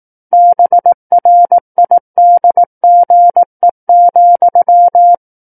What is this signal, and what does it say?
Morse code, heard as a keyed tone, about 18 wpm, 700 Hz. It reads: BRIDGE,
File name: 18 WPM